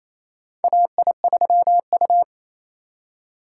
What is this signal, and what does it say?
Morse code, heard as a keyed tone, about 28 wpm, 700 Hz. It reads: AI3U
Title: USA Call Signs - 4 Characters - 2x Character Spacing - 28wpm